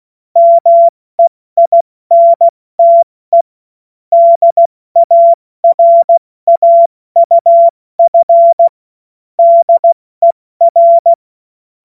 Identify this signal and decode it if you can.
Morse code, heard as a keyed tone, about 16 wpm, spaced as for 14 wpm, 680 Hz. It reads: MEINTE DARAUF DER
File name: Morsekurs / Witze / Witz 21.November 2023 70 F80